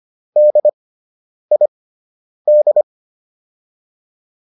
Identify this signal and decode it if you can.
Morse code, heard as a keyed tone, about 25 wpm, spaced as for 9 wpm, 600 Hz. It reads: DID